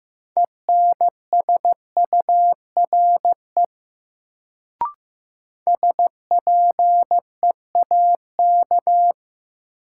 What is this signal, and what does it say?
Morse code, heard as a keyed tone, about 15 wpm, 700 Hz. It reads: ENSURE SPEAK
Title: Sentences from Top 1100 Words - New Word Drill Alt - 15wpm